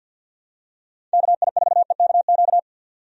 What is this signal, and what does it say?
Morse code, heard as a keyed tone, about 51 wpm, 700 Hz. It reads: KI4EKQ
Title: USA Call Signs - 6 Characters - 50wpm